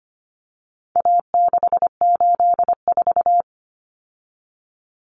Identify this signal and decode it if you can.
Morse code, heard as a keyed tone, about 25 wpm, 700 Hz. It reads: A684
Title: Common Cut Numbers - 4-Digit 25wpm